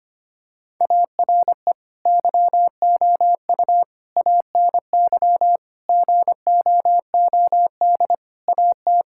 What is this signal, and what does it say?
Morse code, heard as a keyed tone, about 25 wpm, 700 Hz. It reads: ARE YOU ANY GOOD AT
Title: Sentences from Top 1000 Words - Review Part 6 - 25wpm